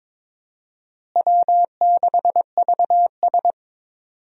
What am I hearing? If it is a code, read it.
Morse code, heard as a keyed tone, about 22 wpm, 700 Hz. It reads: W6VS
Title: USA Call Signs - 4 Characters - Repeated 3x - 22wpm